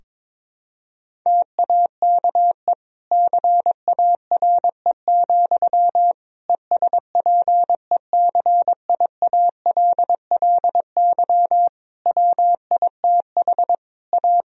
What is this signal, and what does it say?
Morse code, heard as a keyed tone, about 22 wpm, 700 Hz. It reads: TAKE CARE, ESPECIALLY WITH A
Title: Sentences from Top 1000 Words - Review Part 4 - 22wpm